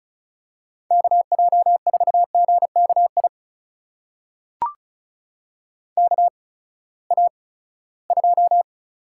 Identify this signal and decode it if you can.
Morse code, heard as a keyed tone, about 36 wpm, spaced as for 20 wpm, 700 Hz. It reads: KJ4GKI KA2
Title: USA Call Signs - 6 Characters - 8x Character Spacing - 35wpm